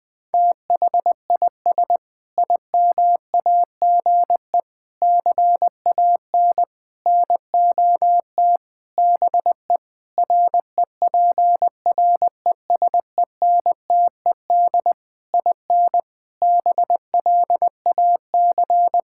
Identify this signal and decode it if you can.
Morse code, heard as a keyed tone, about 21 wpm, 700 Hz. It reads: THIS IMAGE CAN NOT BE REPRESENTED IN BLAC
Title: Sentences with 5-Word Phrases - Spoken Beginning - 20wpm